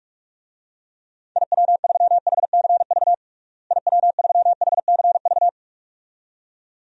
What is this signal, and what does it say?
Morse code, heard as a keyed tone, about 46 wpm, 700 Hz. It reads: IW3HCV IW3HCV
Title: DXCC-100 Call Signs - 6 Characters - Repeated 2x - 45wpm